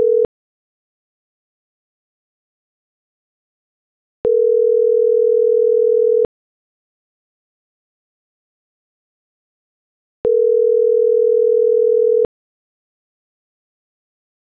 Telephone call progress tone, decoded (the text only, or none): ringback tone